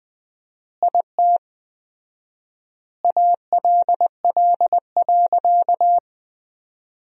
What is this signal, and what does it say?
Morse code, heard as a keyed tone, about 20 wpm, 700 Hz. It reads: IT ALL.
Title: 4x Word-Spacing - Sentences from Top 300 Words - 20wpm